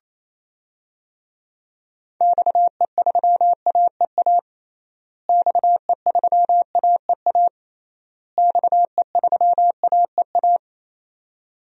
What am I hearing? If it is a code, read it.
Morse code, heard as a keyed tone, about 28 wpm, 700 Hz. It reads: XE3AEA XE3AEA XE3AEA
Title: DXCC-100 Call Signs - 6 Characters - Repeated 3x - 28wpm